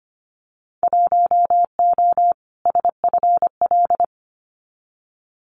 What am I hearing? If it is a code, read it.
Morse code, heard as a keyed tone, about 25 wpm, 700 Hz. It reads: 1O SFL
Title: Winter Field Day Exchanges 25wpm